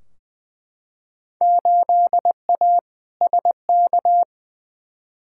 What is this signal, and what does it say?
Morse code, heard as a keyed tone, about 20 wpm, 700 Hz. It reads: 8A SK
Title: Field Day Exchanges Encore - 20wpm